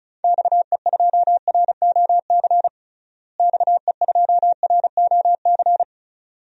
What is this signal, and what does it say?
Morse code, heard as a keyed tone, about 36 wpm, 700 Hz. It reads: XE2ROC XE2ROC
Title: DXCC-100 Call Signs - 6 Characters - Repeated 2x - 35wpm